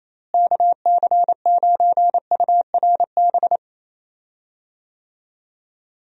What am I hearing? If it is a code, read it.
Morse code, heard as a keyed tone, about 28 wpm, 700 Hz. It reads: KC9URB
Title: USA Call Signs - 6 Characters - Repeated 2x - 28wpm